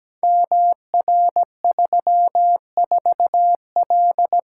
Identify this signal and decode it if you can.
Morse code, heard as a keyed tone, about 17 wpm, 700 Hz. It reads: MR34L
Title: Code Groups - 5 Letter:Numbers - 17wpm